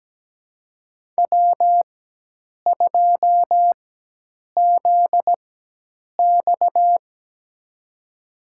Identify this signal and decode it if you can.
Morse code, heard as a keyed tone, about 17 wpm, spaced as for 8 wpm, 700 Hz. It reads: W2ZX
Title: USA Call Signs - 4x Character Spacing - 17wpm